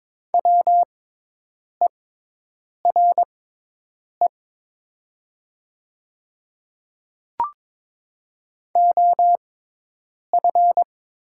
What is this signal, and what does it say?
Morse code, heard as a keyed tone, about 22 wpm, spaced as for 8 wpm, 700 Hz. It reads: WERE OF
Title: 6x Character Spacing - Top 100 Words - 22wpm